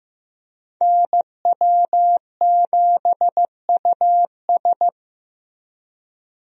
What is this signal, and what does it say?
Morse code, heard as a keyed tone, about 15 wpm, 700 Hz. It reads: NW7US
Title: USA Call Signs - 5 Characters - Repeated 3x - 15wpm